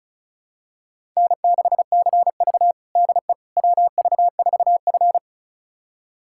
Morse code, heard as a keyed tone, about 36 wpm, 700 Hz. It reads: N6CV DE WV4F
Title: USA Callsign DE Callsign - 4 Characters - 35wpm